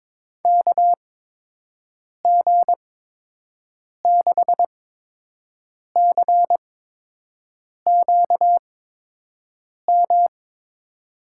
Morse code, heard as a keyed tone, about 23 wpm, spaced as for 6 wpm, 700 Hz. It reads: KG6CQM
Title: USA Call Signs - 6 Characters - 8x Character Spacing - 22wpm